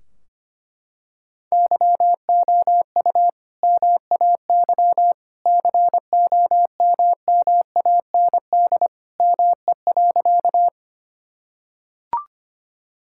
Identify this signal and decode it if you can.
Morse code, heard as a keyed tone, about 25 wpm, 700 Hz. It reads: YOU MAY COMMAND ME.
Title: Sentences from Top 1600 Words 25wpm